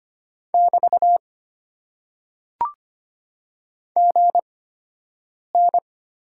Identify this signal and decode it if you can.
Morse code, heard as a keyed tone, about 25 wpm, spaced as for 7 wpm, 700 Hz. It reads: <BT> GN
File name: Top 90 QSO Elements - 8x Character Spacing - 25wpm